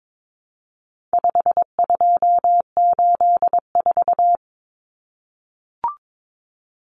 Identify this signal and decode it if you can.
Morse code, heard as a keyed tone, about 22 wpm, 700 Hz. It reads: 5284
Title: Common Cut Numbers - 4-Digit 22wpm